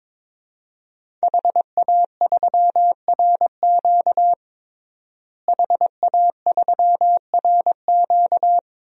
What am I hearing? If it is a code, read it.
Morse code, heard as a keyed tone, about 22 wpm, 700 Hz. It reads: HA3RQ HA3RQ
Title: DXCC-100 Call Signs - 5 Characters - Repeated 2x - 22wpm